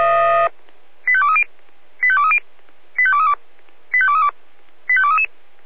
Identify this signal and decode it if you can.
Five-tone selective call (ZVEI), heard as quick runs of five tones, tones of about 70 ms. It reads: 85230 85230 85232 85232 85233